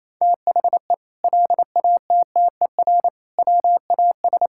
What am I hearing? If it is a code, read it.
Morse code, heard as a keyed tone, about 28 wpm, 700 Hz. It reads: THE LATTER WAS
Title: Sentences from Top 1600 Words 28wpm